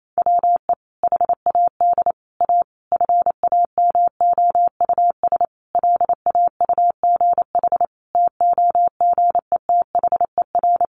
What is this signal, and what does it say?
Morse code, heard as a keyed tone, about 28 wpm, 700 Hz. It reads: WE HAD A FAMOUS LAUGH TOGETHER
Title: Sentences from Top 1500 Words 28wpm